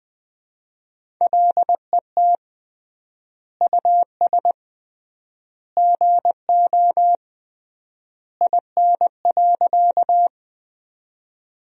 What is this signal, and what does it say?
Morse code, heard as a keyed tone, about 21 wpm, 700 Hz. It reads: LET US GO IN.
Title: 3x Word-Spacing - Sentences from Top 300 Words - 20wpm